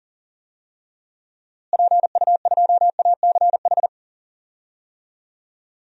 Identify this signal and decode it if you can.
Morse code, heard as a keyed tone, about 40 wpm, 700 Hz. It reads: PU2ACH
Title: DXCC-100 Call Signs - 6 Characters - Repeated 3x - 40wpm